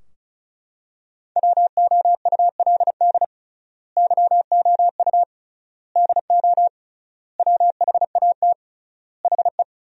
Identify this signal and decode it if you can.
Morse code, heard as a keyed tone, about 34 wpm, 700 Hz. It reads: WOULD YOU DO WHAT HE
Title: 3x Word-Spacing - Sentences from Top 100 Words - 35wpm